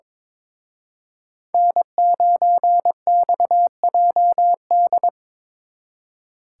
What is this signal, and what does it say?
Morse code, heard as a keyed tone, about 25 wpm, 700 Hz. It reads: N9XJD
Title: Speed Racing - USA 1x3 Call Signs 22wpm